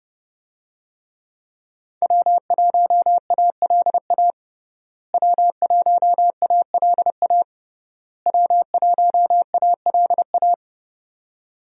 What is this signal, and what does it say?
Morse code, heard as a keyed tone, about 30 wpm, 700 Hz. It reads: W1ALA W1ALA W1ALA
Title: USA Call Signs - 5 Characters - Repeated 3x - 30wpm